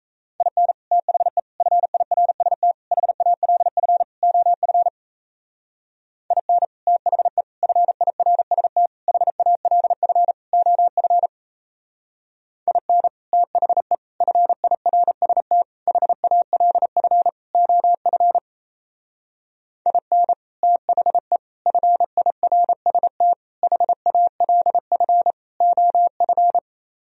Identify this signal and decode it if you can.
Morse code, heard as a keyed tone, about 35 wpm, 700 Hz. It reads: IN THE FIRST HALF OF IN THE FIRST HALF OF IN THE FIRST HALF OF IN THE FIRST HALF OF
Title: Speed-Racing - Sets of 5 Words 28wpm